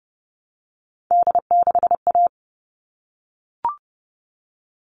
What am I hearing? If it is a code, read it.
Morse code, heard as a keyed tone, about 30 wpm, 700 Hz. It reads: D6A
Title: All Cut Numbers - 3-Digit 30wpm